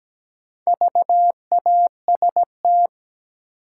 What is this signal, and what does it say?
Morse code, heard as a keyed tone, about 17 wpm, 700 Hz. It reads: VAST
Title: Sentences from Top 1300 Words - New Word Drill - 17wpm